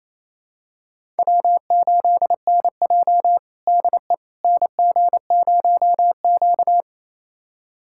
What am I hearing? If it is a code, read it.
Morse code, heard as a keyed tone, about 28 wpm, 700 Hz. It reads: W8NJ DE NG0Q
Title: USA Callsign DE Callsign - 4 Characters - 28wpm